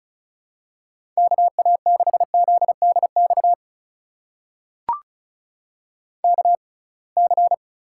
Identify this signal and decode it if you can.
Morse code, heard as a keyed tone, about 34 wpm, spaced as for 26 wpm, 700 Hz. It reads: KA6ZDX KC
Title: USA Call Signs - 6 Characters - 6x Character Spacing - 35wpm